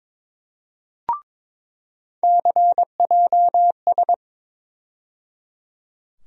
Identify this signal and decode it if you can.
Morse code, heard as a keyed tone, about 22 wpm, 700 Hz. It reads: CJS
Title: USA Call Sign Postfixes - 3 Letters 22wpm